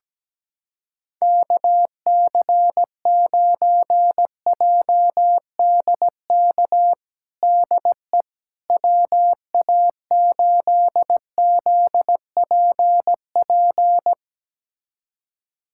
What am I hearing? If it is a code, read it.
Morse code, heard as a keyed tone, about 17 wpm, 700 Hz. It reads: KC9JDK DE WA8ZPP
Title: USA Callsign DE Callsign - 6 Characters - 17wpm